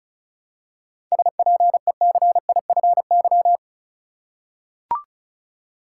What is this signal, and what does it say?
Morse code, heard as a keyed tone, about 36 wpm, 700 Hz. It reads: SPECIFY
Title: Seven Letter Words - 35wpm